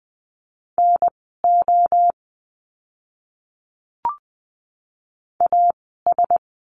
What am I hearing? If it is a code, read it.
Morse code, heard as a keyed tone, about 20 wpm, spaced as for 15 wpm, 700 Hz. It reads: NO AS
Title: 054 Lesson 10f - Letter r - Words 20wpm